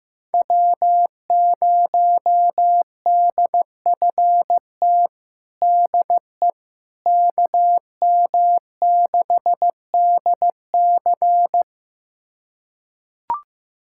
Morse code, heard as a keyed tone, about 15 wpm, 700 Hz. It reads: W0DFT DE KM6DC
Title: USA Callsign DE Callsign - 5 Characters - 15wpm